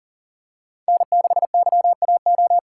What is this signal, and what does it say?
Morse code, heard as a keyed tone, about 40 wpm, 700 Hz. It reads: N6YAO
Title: USA Call Signs - 5 Characters - 3x Character Spacing - 40wpm